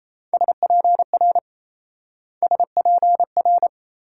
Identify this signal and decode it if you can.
Morse code, heard as a keyed tone, about 34 wpm, 700 Hz. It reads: SPR SPR
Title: Speed-Racing Multi-letter Phonemes - 28wpm